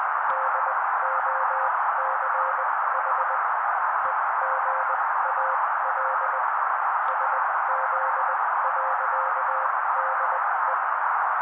Morse code, heard as a keyed tone, about 20 wpm, spaced as for 16 wpm, 530 Hz. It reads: DOCH EGAL SZ.DE
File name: daily / 16wpm noise